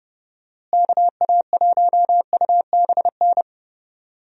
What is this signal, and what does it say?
Morse code, heard as a keyed tone, about 30 wpm, 700 Hz. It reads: KA1UBN
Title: USA Call Signs - 6 Characters - Repeated 2x - 30wpm